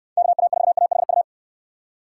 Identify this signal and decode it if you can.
Morse code, heard as a keyed tone, about 69 wpm, 700 Hz. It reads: KA4AHU